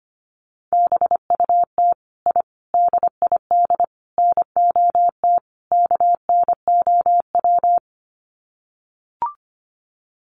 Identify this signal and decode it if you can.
Morse code, heard as a keyed tone, about 25 wpm, 700 Hz. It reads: BUT I DID NOT KNOW
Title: Sets of 5 Words Encore 25wpm